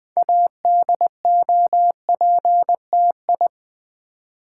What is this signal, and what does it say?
Morse code, heard as a keyed tone, about 20 wpm, 700 Hz. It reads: ADOPTI
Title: Build A Word - 3 Letters At A Time - 8-Letter Words - 20wpm